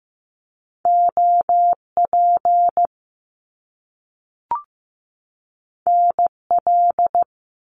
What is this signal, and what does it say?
Morse code, heard as a keyed tone, about 15 wpm, 700 Hz. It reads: OP NL
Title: Top Two Letter Combinations in Words 15wpm